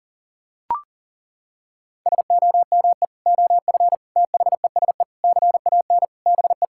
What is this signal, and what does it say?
Morse code, heard as a keyed tone, about 40 wpm, 700 Hz. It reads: SOME OF THESE CAN BE
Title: Sentences from Top 1000 Words - Review Part 6 - 40wpm